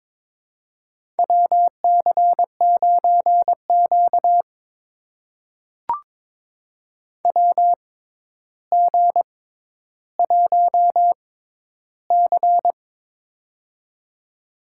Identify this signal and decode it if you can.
Morse code, heard as a keyed tone, about 23 wpm, spaced as for 11 wpm, 700 Hz. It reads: WC9Q WG1C
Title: USA Call Signs - 4 Characters - 6x Character Spacing - 22wpm